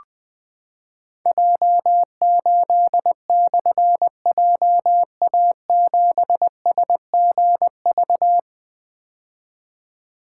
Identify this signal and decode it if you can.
Morse code, heard as a keyed tone, about 20 wpm, 700 Hz. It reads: J8/JA7SGV
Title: Difficult Call Signs 20wpm